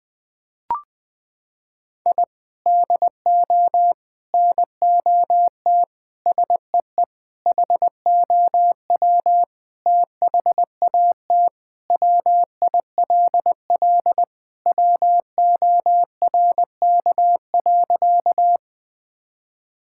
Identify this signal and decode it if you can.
Morse code, heard as a keyed tone, about 20 wpm, 700 Hz. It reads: I DO NOT SEE HOW THAT WILL WORK.
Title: Sentences with 5-Word Phrases - 20wpm